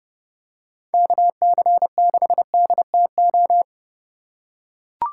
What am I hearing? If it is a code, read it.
Morse code, heard as a keyed tone, about 30 wpm, 700 Hz. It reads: KC6DTO